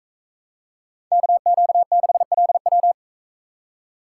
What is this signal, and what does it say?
Morse code, heard as a keyed tone, about 42 wpm, 700 Hz. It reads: KQ6LW